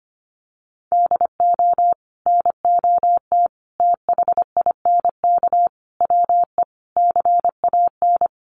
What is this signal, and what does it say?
Morse code, heard as a keyed tone, about 25 wpm, 700 Hz. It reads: DO NOT THINK WE CAN
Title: Sets of 5 Words Encore2 25wpm